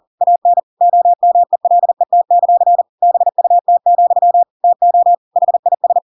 Morse code, heard as a keyed tone, about 40 wpm, 700 Hz. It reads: AN OMELET; BUT, TO HIS
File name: Pinocchio / Pinocchio ch5 40 40 ik1hge